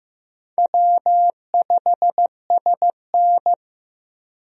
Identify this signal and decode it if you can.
Morse code, heard as a keyed tone, about 16 wpm, 700 Hz. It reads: W5SN
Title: USA Call Signs - 4 Characters - Repeated 2x - 15wpm